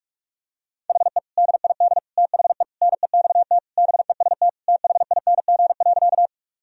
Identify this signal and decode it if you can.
Morse code, heard as a keyed tone, about 46 wpm, 700 Hz. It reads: HE DID THE NEXT BEST THING.
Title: Sentences from Top 1000 Words - Review Part 1 - 45wpm